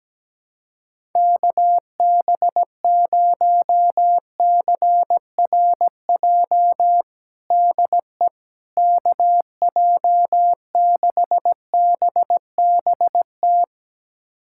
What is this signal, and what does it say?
Morse code, heard as a keyed tone, about 17 wpm, 700 Hz. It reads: KB0CRJ DE KJ6BBT
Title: USA Callsign DE Callsign - 6 Characters - 17wpm